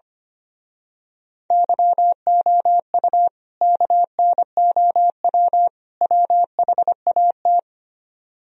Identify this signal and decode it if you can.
Morse code, heard as a keyed tone, about 25 wpm, 700 Hz. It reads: YOU KNOW WHAT
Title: Speed-Racing - Sets of 3 Words 25wpm